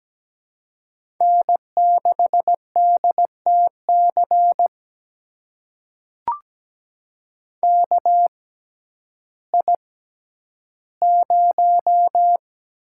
Morse code, heard as a keyed tone, about 17 wpm, spaced as for 10 wpm, 700 Hz. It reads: N6DTC KI0
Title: USA Call Signs - 5 Characters - 6x Character Spacing - 17wpm